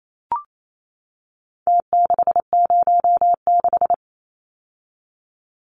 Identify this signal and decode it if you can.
Morse code, heard as a keyed tone, about 28 wpm, 700 Hz. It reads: T606
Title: Numbers - Contest Style - 4-Digit 28wpm